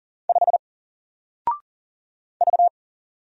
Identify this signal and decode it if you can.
Morse code, heard as a keyed tone, about 40 wpm, 700 Hz. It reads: 5 V
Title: Single Character (Letter:Number:Punctuation:Prosign) 40wpm